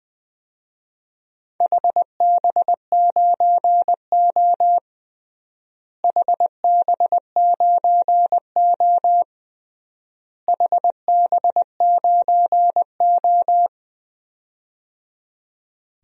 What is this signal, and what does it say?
Morse code, heard as a keyed tone, about 20 wpm, 700 Hz. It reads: HB9O HB9O HB9O
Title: DXCC-100 Call Signs - 4 Characters - Repeated 3x - 20wpm